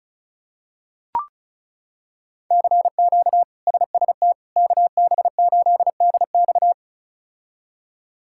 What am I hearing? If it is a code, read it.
Morse code, heard as a keyed tone, about 36 wpm, 700 Hz. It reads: CQ SST KB8DX
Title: K1USN SST QSO Practice - 35wpm